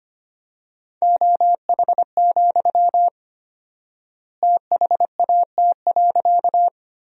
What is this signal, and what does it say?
Morse code, heard as a keyed tone, about 25 wpm, 700 Hz. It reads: OH, THAT.
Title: 4x Word-Spacing - Sentences from Top 500 Words - 25wpm